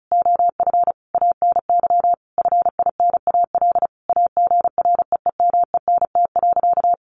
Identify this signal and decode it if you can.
Morse code, heard as a keyed tone, about 34 wpm, 700 Hz. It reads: OF ANY FINAL AGREEMENT.
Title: Sentences from Top 1500 Words 35wpm